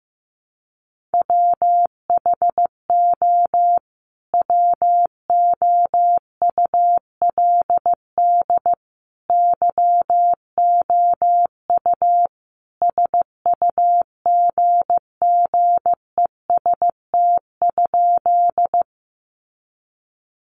Morse code, heard as a keyed tone, about 14 wpm, 700 Hz. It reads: WHO WOULD YOU SUGGEST?
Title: Sentences from Top 1300 Words 15wpm